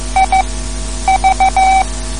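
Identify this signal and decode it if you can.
Morse code, heard as a keyed tone, about 14 wpm, spaced as for 9 wpm, 770 Hz. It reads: IV
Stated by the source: Morse code lesson, third lesson in a series for learning the Morse alphabet.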